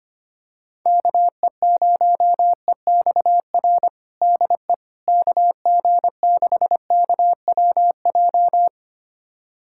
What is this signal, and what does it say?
Morse code, heard as a keyed tone, about 25 wpm, 700 Hz. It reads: KE0EXR DE KG6KWJ
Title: USA Callsign DE Callsign - 6 Characters - 25wpm